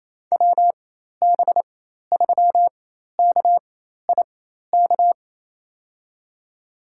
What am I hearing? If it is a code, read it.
Morse code, heard as a keyed tone, about 28 wpm, spaced as for 13 wpm, 700 Hz. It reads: WB3KIK
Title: USA Call Signs - 4x Character Spacing - 28wpm